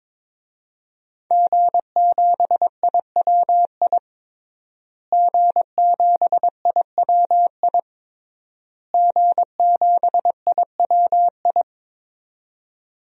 Morse code, heard as a keyed tone, about 22 wpm, 700 Hz. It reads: G7IWI G7IWI G7IWI
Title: DXCC-100 Call Signs - 5 Characters - Repeated 3x - 22wpm